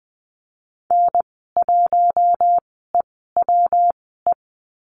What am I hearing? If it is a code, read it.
Morse code, heard as a keyed tone, about 20 wpm, spaced as for 15 wpm, 700 Hz. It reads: N1EWE
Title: 154 Lesson 25g - Letter p - Call Signs 20wpm